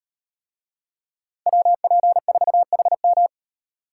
Morse code, heard as a keyed tone, about 38 wpm, 700 Hz. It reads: WP4HM